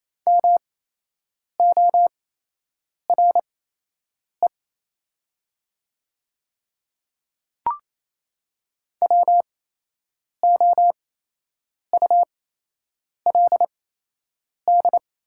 Morse code, heard as a keyed tone, about 28 wpm, spaced as for 8 wpm, 700 Hz. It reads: MORE WOULD
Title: 8x Character Spacing - Top 100 Words - 28wpm